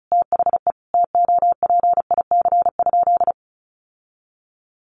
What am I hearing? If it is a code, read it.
Morse code, heard as a keyed tone, about 34 wpm, 700 Hz. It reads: THE TOPIC?
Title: Sentences from Top 1200 Words 35wpm